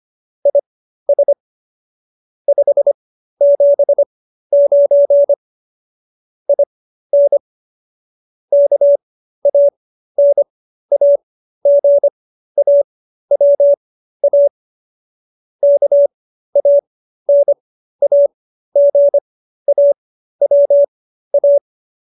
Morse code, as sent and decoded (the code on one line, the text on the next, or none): .. ... / ..... --... ----. / .. -. / -.- .- -. .- --. .- .-- .- / -.- .- -. .- --. .- .-- .-
IS 579 IN KANAGAWA KANAGAWA